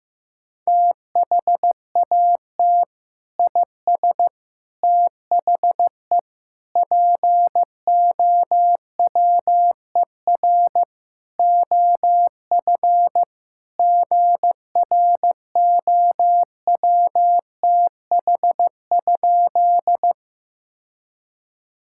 Morse code, heard as a keyed tone, about 16 wpm, 700 Hz. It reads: THAT IS THE POWER OF GROWTH?
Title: Sentences from Top 1000 Words - Review Part 5 - 15wpm